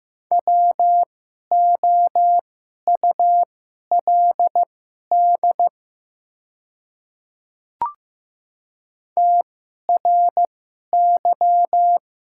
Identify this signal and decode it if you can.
Morse code, heard as a keyed tone, about 15 wpm, spaced as for 11 wpm, 690 Hz. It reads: WOULD TRY
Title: 2x Character Spacing - Top 500 Words - 15wpm